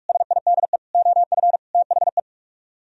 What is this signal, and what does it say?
Morse code, heard as a keyed tone, about 59 wpm, spaced as for 52 wpm, 700 Hz. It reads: SIDE OF THE
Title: Speed-Racing - Sets of 4 Words 45wpm